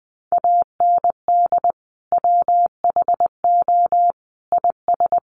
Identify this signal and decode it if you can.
Morse code, heard as a keyed tone, about 20 wpm, 700 Hz. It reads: AND WHO IS
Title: Sentences from Top 200 Words 20wpm